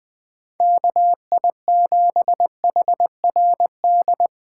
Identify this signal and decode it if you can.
Morse code, heard as a keyed tone, about 20 wpm, 700 Hz. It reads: KI7HRD
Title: USA Call Signs - 6 Characters - 2x Character Spacing - 20wpm